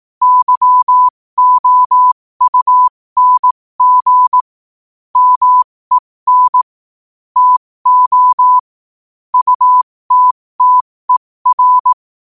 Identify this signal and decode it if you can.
Morse code, heard as a keyed tone, about 18 wpm, spaced as for 15 wpm, 1000 Hz. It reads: YOUNG MEN TO UTTER
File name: qotdfiles / 15wpm